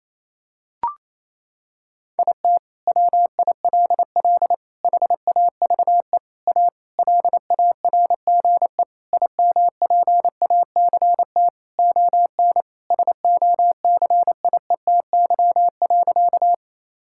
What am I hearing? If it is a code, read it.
Morse code, heard as a keyed tone, about 27 wpm, 700 Hz. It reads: IT WILL HAVE A LARGE IMPACT ON SOCIETY.
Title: Sentences from Top 1000 Words - Review Part 6 - 28wpm